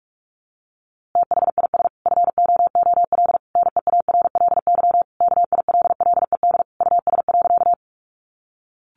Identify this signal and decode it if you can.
Morse code, heard as a keyed tone, about 44 wpm, 700 Hz. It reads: THIS FOOL NEARLY KILLED US.
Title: A Sound Of Thunder 45wpm